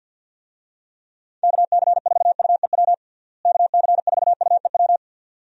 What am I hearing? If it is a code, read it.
Morse code, heard as a keyed tone, about 51 wpm, 700 Hz. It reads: KC4UEW KC4UEW
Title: USA Call Signs - 6 Characters - Repeated 2x - 50wpm